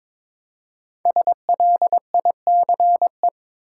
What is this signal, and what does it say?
Morse code, heard as a keyed tone, about 22 wpm, 700 Hz. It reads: SLICE
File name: 5 Letter Words - First 3 Letters Given - 22wpm